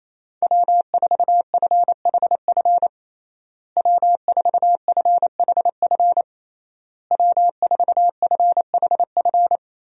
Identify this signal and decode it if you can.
Morse code, heard as a keyed tone, about 28 wpm, 700 Hz. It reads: W4FHF W4FHF W4FHF
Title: USA Call Signs - 5 Characters - Repeated 3x - 28wpm